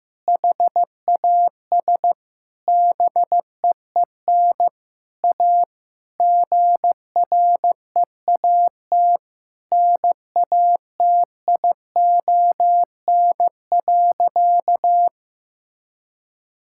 Morse code, heard as a keyed tone, about 16 wpm, 700 Hz. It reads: HAS BEEN A GREAT NATION.
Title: Sentences from Top 1000 Words - Review Part 6 - 15wpm